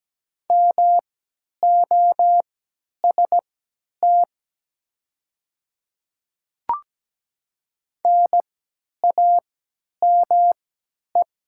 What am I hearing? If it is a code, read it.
Morse code, heard as a keyed tone, about 17 wpm, spaced as for 10 wpm, 700 Hz. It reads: MOST NAME
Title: 3x Character Spacing - Top 200 Words - 17wpm